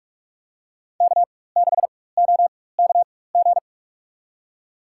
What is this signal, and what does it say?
Morse code, heard as a keyed tone, about 46 wpm, spaced as for 21 wpm, 700 Hz. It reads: K6OKG
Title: USA Call Signs - 5 Characters - 4x Character Spacing - 45wpm